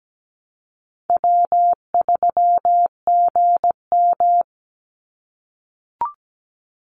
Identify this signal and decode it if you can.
Morse code, heard as a keyed tone, about 17 wpm, 700 Hz. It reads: W3GM
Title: USA Call Signs 17wpm